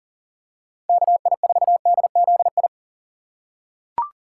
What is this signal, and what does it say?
Morse code, heard as a keyed tone, about 40 wpm, 700 Hz. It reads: KI4DZI
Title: USA Call Signs - 6x Character Spacing - 40wpm